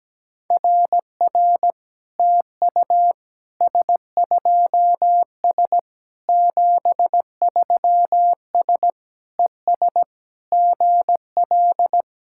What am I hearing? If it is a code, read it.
Morse code, heard as a keyed tone, about 17 wpm, 700 Hz. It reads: RR TU S2S 73S ES GL
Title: SOTA QSO Practice 17wpm